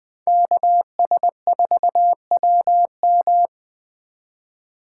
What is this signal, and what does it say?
Morse code, heard as a keyed tone, about 21 wpm, 700 Hz. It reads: KS4WM
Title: USA Call Signs - 5 Characters - 4x Character Spacing - 20wpm